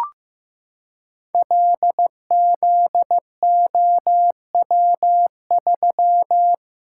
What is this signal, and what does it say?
Morse code, heard as a keyed tone, about 15 wpm, 700 Hz. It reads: LZOW3
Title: Code Groups - 5 Letter:Numbers - 15wpm